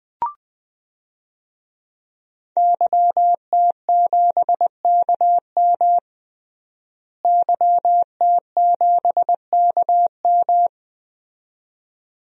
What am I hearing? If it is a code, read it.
Morse code, heard as a keyed tone, about 20 wpm, 700 Hz. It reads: YT7KM YT7KM
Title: DXCC-100 Call Signs - 5 Characters - Repeated 2x - 20wpm